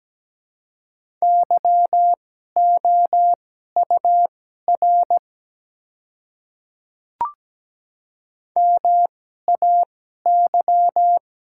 Morse code, heard as a keyed tone, about 17 wpm, spaced as for 12 wpm, 700 Hz. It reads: YOUR MAY
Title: 2x Character Spacing - Top 200 Words - 17wpm